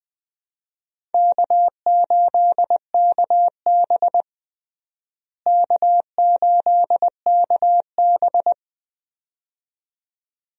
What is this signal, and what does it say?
Morse code, heard as a keyed tone, about 21 wpm, 700 Hz. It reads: K8KB K8KB
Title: USA Call Signs - 4 Characters - Repeated 2x - 20wpm